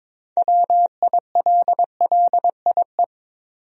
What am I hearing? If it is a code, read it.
Morse code, heard as a keyed tone, about 22 wpm, 700 Hz. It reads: WILLIE
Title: USA Common Names - 22wpm